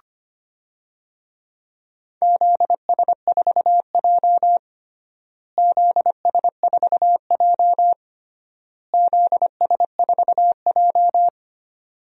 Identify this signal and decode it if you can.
Morse code, heard as a keyed tone, about 25 wpm, 700 Hz. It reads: ZS4J ZS4J ZS4J
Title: DXCC-100 Call Signs - 4 Characters - Repeated 3x - 25wpm